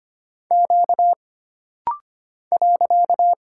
Morse code, heard as a keyed tone, about 25 wpm, 700 Hz. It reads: Q .
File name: Single Character (Letter:Number:Punctuation:Prosign) - Rapid-Fire - 25wpm